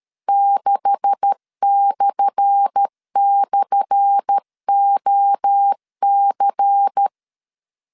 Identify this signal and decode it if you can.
Morse code, heard as a keyed tone, about 13 wpm, 800 Hz. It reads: 6//OC
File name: LEZIONI / 10 SIGNS / 10A SLASH 12wpm 2314